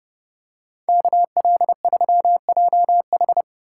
Morse code, heard as a keyed tone, about 30 wpm, 700 Hz. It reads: KL3JH